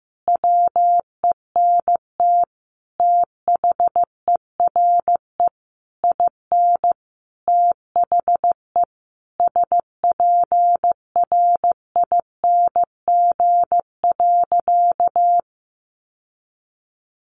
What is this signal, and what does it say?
Morse code, heard as a keyed tone, about 15 wpm, 700 Hz. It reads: WENT THERE IN THE SPRING.
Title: Sentences from Top 1200 Words 15wpm